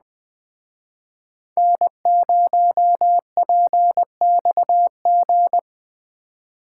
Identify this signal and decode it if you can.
Morse code, heard as a keyed tone, about 20 wpm, 700 Hz. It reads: N0PXG